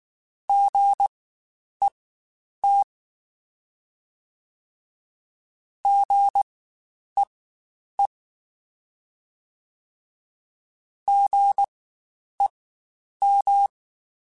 Morse code, heard as a keyed tone, about 19 wpm, spaced as for 9 wpm, 800 Hz. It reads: GET GEE GEM
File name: Spaced Alphas / Spaced Alphas mp3s / Gs sp @18